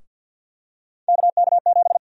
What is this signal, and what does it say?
Morse code, heard as a keyed tone, about 52 wpm, 700 Hz. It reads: KK7
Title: USA Call Sign Prefixes - Repeated 2x - 50wpm